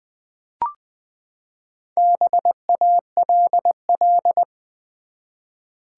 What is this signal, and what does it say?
Morse code, heard as a keyed tone, about 20 wpm, 700 Hz. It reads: BALL
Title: Four Letter Words - 20wpm